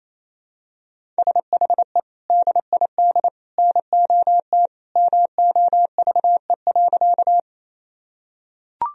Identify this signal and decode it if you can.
Morse code, heard as a keyed tone, about 28 wpm, 700 Hz. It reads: SHE DID NOT MOVE.
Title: Sentences from Top 1000 Words - Review Part 1 - 28wpm